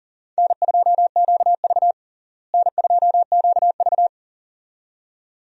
Morse code, heard as a keyed tone, about 40 wpm, 700 Hz. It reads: N2QV N2QV
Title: USA Call Signs - 4 Characters - Repeated 2x - 40wpm